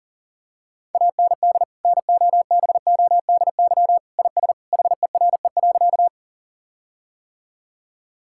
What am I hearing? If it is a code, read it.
Morse code, heard as a keyed tone, about 40 wpm, 700 Hz. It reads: AND NOBODY IS HERE.
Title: Sentences from Top 1600 Words 40wpm